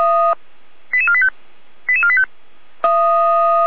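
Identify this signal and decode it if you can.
Five-tone selective call (ZVEI), heard as quick runs of five tones, tones of about 70 ms. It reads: 88485 88485
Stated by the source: ZVEI five-tone selective call received on an SDR.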